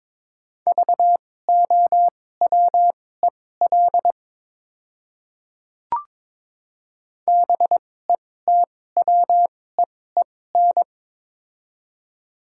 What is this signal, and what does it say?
Morse code, heard as a keyed tone, about 22 wpm, spaced as for 16 wpm, 700 Hz. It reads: VOWEL BETWEEN
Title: 2x Character Spacing - Top 500 Words - 22wpm